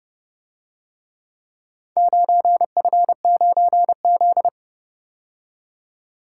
Morse code, heard as a keyed tone, about 29 wpm, 700 Hz. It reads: 9F9Z